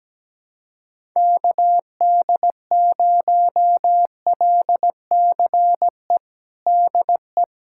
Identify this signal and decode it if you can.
Morse code, heard as a keyed tone, about 17 wpm, 700 Hz. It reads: KD0LCE DE
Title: USA Callsign DE Callsign - 6 Characters - 17wpm